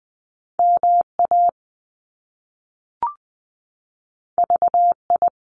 Morse code, heard as a keyed tone, about 20 wpm, 700 Hz. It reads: M A V I
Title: Top Two Letter Combinations in Words 20wpm